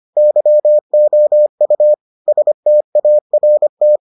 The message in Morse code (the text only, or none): YOU START